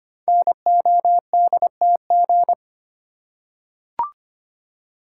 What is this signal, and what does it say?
Morse code, heard as a keyed tone, about 25 wpm, 700 Hz. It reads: NODTG
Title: Code Groups - 5 Letters - 25wpm